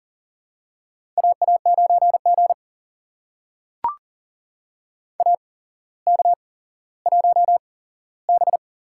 Morse code, heard as a keyed tone, about 40 wpm, spaced as for 17 wpm, 700 Hz. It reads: AA9G AK1B
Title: USA Call Signs - 4 Characters - 8x Character Spacing - 40wpm